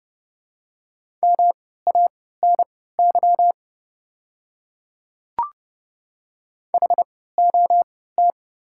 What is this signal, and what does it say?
Morse code, heard as a keyed tone, about 32 wpm, spaced as for 17 wpm, 690 Hz. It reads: MANY HOT
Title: 3x Character Spacing - Top 100 Words - 30wpm